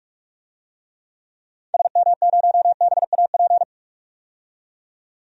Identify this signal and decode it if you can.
Morse code, heard as a keyed tone, about 46 wpm, 700 Hz. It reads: SM0BAP
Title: DXCC-100 Call Signs - 6 Characters - 45wpm